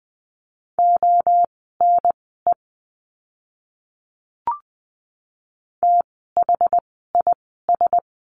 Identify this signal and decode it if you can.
Morse code, heard as a keyed tone, about 20 wpm, spaced as for 14 wpm, 700 Hz. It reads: ONE THIS
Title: 192 Lesson 31f - Letter v - Words 20wpm